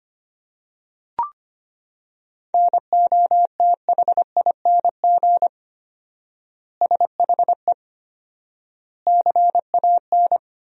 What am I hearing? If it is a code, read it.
Morse code, heard as a keyed tone, about 25 wpm, 700 Hz. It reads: NOTHING SHE CAN
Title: 4x Word-Spacing - Sentences from Top 500 Words - 25wpm